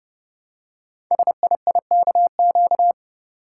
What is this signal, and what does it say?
Morse code, heard as a keyed tone, about 30 wpm, 700 Hz. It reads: SIIKQ
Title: Code Groups - 5 Letters - 30wpm